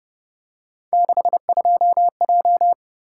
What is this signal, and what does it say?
Morse code, heard as a keyed tone, about 29 wpm, 700 Hz. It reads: 62J